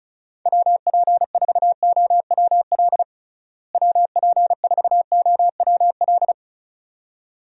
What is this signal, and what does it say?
Morse code, heard as a keyed tone, about 36 wpm, 700 Hz. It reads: WP4OWL WP4OWL
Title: USA Call Signs - 6 Characters - Repeated 2x - 35wpm